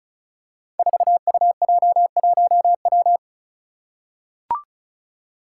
Morse code, heard as a keyed tone, about 36 wpm, 700 Hz. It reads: 4UJ1W
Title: Five Challenging Letter-Number Combinations 35wpm